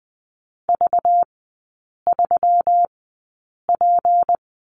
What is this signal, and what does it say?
Morse code, heard as a keyed tone, about 20 wpm, spaced as for 8 wpm, 690 Hz. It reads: V3P
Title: 212 Lesson 34e - Number 8 - Three Characters 20wpm